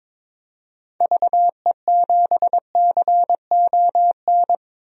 Morse code, heard as a keyed tone, about 22 wpm, 700 Hz. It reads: VE7CON
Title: DXCC-100 Call Signs - 6 Characters - Repeated 2x - 22wpm